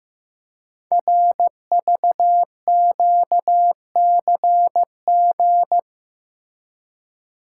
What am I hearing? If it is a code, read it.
Morse code, heard as a keyed tone, about 15 wpm, 700 Hz. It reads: RVQCG
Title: Code Groups - 5 Letters - 15wpm